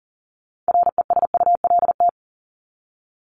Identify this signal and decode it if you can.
Morse code, heard as a keyed tone, about 40 wpm, 700 Hz. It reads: RESULT
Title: Top 501-750 Words 40wpm